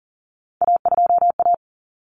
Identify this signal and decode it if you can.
Morse code, heard as a keyed tone, about 40 wpm, 700 Hz. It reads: A2A
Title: Common Cut Numbers - 3-Digit 40wpm